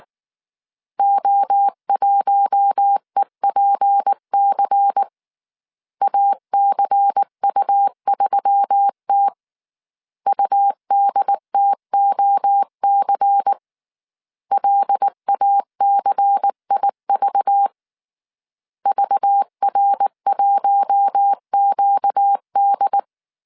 Morse code, heard as a keyed tone, about 19 wpm, 800 Hz. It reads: O1EPC ACU3T UDTOC LACIV VR1QD
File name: CW / DET / 1 6 1500C S C 18wpm 2356